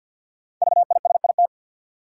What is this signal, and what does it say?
Morse code, heard as a keyed tone, about 53 wpm, 700 Hz. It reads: VISIT
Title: Sentences from Top 900 Words - New Word Drill - 50wpm